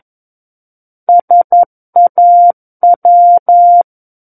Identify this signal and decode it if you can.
Morse code, heard as a keyed tone, about 11 wpm, 700 Hz. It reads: SAW
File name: Book One / 02-Chapter 2 - 11 WPM